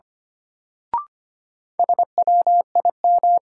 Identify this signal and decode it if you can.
Morse code, heard as a keyed tone, about 25 wpm, 700 Hz. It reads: SWIM
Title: Sentences from Top 1000 Words - Review Part 6 - New Word Drill - 25wpm